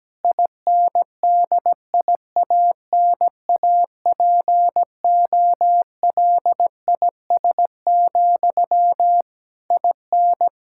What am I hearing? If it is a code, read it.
Morse code, heard as a keyed tone, about 17 wpm, 700 Hz. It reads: INDIANAPOLIS, IN
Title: Practice QSO - Exchange 1 - Beginning - 17wpm